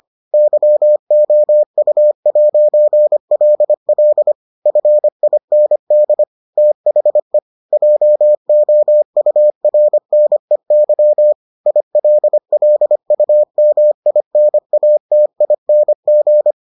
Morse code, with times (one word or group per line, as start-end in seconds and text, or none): 0.3-4.3 YOU'LL
4.7-6.2 FIND
6.6-7.4 THE
7.7-11.3 JOURNEY
11.7-16.5 ILLUMINATING